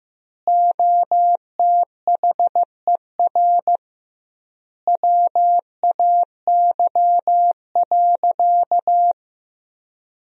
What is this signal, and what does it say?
Morse code, heard as a keyed tone, about 16 wpm, 700 Hz. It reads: OTHER WAY.
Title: 2x Word-Spacing - Sentences from Top 100 Words - 15wpm